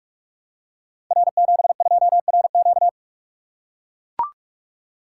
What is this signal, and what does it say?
Morse code, heard as a keyed tone, about 44 wpm, 700 Hz. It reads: R72RQ
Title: Code Groups - 5 Letter:Numbers - 45wpm